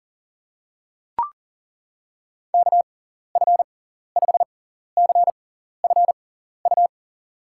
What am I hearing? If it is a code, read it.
Morse code, heard as a keyed tone, about 40 wpm, spaced as for 14 wpm, 700 Hz. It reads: KF5CFU
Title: USA Call Signs - 6 Characters - 6x Character Spacing - 40wpm